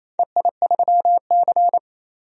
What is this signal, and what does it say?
Morse code, heard as a keyed tone, about 28 wpm, 700 Hz. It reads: EI3C
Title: DXCC-100 Call Signs - 4 Characters - 28wpm